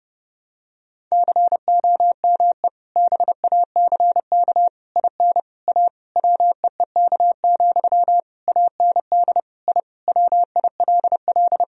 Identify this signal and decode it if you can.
Morse code, heard as a keyed tone, about 31 wpm, 700 Hz. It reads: COME BACK IN A WEEK, AND I WILL
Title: Sentences from Top 1000 Words - Review Part 3 - 30wpm